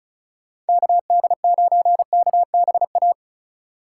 Morse code, heard as a keyed tone, about 36 wpm, 700 Hz. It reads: KD9KBA
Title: USA Call Signs - 3x Character Spacing - 35wpm